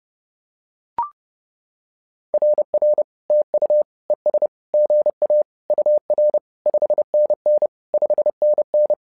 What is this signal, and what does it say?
Morse code, heard as a keyed tone, about 30 wpm, 600 Hz. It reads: RR TU ES GA UR 5NN 5NN
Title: SOTA QSO Practice 30wpm